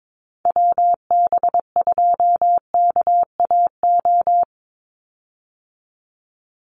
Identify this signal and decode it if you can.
Morse code, heard as a keyed tone, about 22 wpm, 700 Hz. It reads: WB2KAO
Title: Call Signs Repeated 3x - 22wpm